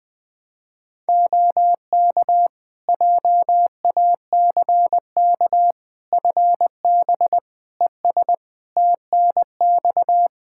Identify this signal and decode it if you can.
Morse code, heard as a keyed tone, about 20 wpm, 700 Hz. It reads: OK JACK FB ES TNX
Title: Practice QSO - Exchange 2 - Info - 20wpm